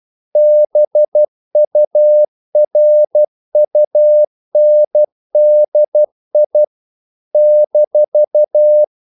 Morse code, heard as a keyed tone, about 12 wpm, 600 Hz. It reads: BURUNDI -
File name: CW / Practice / NationalCapitals12